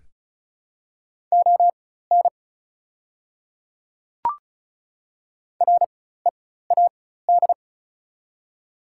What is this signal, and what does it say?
Morse code, heard as a keyed tone, about 36 wpm, spaced as for 16 wpm, 700 Hz. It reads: ON READ
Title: 4x Character Spacing - Top 200 Words - 35wpm